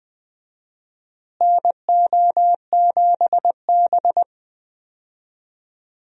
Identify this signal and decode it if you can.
Morse code, heard as a keyed tone, about 20 wpm, 700 Hz. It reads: NO7B
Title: USA Call Signs - 4 Characters - Repeated 2x - 20wpm